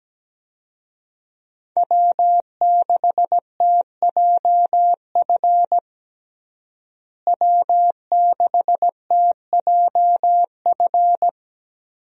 Morse code, heard as a keyed tone, about 17 wpm, 700 Hz. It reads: W6TJF W6TJF
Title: USA Call Signs - 5 Characters - Repeated 3x - 17wpm